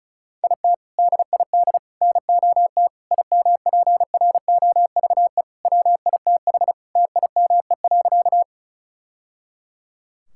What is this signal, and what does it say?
Morse code, heard as a keyed tone, about 36 wpm, 700 Hz. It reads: IT DID NOT IMPROVE WITH TIME.
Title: Sentences from Top 1000 Words - Review Part 5 - 35wpm